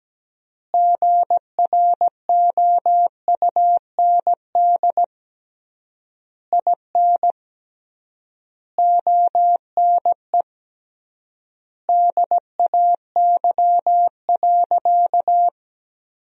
Morse code, heard as a keyed tone, about 17 wpm, 700 Hz. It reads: GROUND IN ONE DAY.
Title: 3x Word-Spacing - Sentences from Top 500 Words - 17wpm